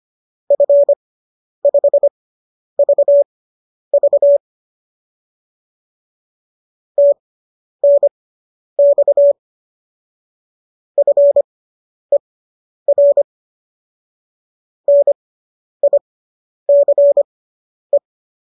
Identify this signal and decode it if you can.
Morse code, heard as a keyed tone, about 25 wpm, spaced as for 10 wpm, 570 Hz. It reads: F5VV TNX FER NICE